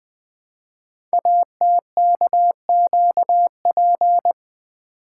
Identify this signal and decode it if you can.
Morse code, heard as a keyed tone, about 20 wpm, 700 Hz. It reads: ATKQP
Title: Code Groups - 5 Letters - 20wpm